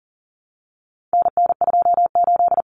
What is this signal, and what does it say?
Morse code, heard as a keyed tone, about 40 wpm, 700 Hz. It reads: NN28
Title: Common Cut Numbers - 4-Digit 40wpm